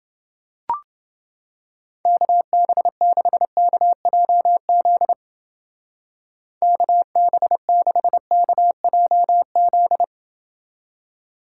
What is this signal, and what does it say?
Morse code, heard as a keyed tone, about 29 wpm, 700 Hz. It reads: KB6KJZ KB6KJZ